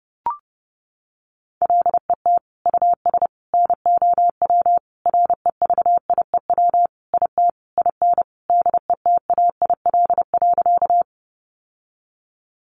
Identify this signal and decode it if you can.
Morse code, heard as a keyed tone, about 30 wpm, 700 Hz. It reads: LET US NOW REVIEW IT IN DETAIL.
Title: Sentences from Top 1500 Words 30wpm